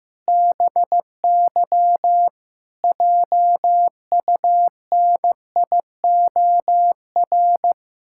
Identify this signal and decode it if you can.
Morse code, heard as a keyed tone, about 15 wpm, 700 Hz. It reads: BY JUNIOR
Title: Pangram Challenge v2 - 15wpm